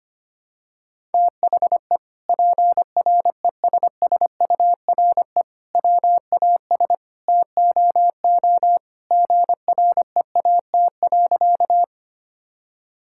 Morse code, heard as a keyed tone, about 25 wpm, 700 Hz. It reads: THE PRESSURE WAS TOO GREAT.
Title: Sentences from Top 1000 Words - Review Part 6 - 25wpm